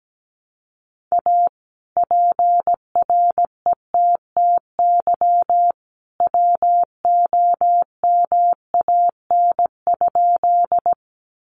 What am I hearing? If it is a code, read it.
Morse code, heard as a keyed tone, about 17 wpm, 700 Hz. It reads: A PRETTY WOMAN?
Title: Sentences from Top 1100 Words 17wpm